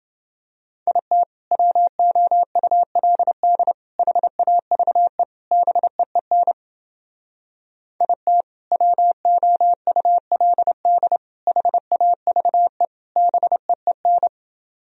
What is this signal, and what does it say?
Morse code, heard as a keyed tone, about 30 wpm, 700 Hz. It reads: IT WOULD HAVE BEEN IT WOULD HAVE BEEN